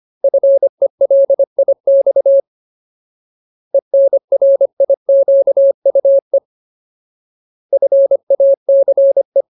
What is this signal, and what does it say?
Morse code, heard as a keyed tone, about 25 wpm, 550 Hz. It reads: FELIX ENRIQUE FACE
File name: Ultimate Word List / 25WPM Ultimate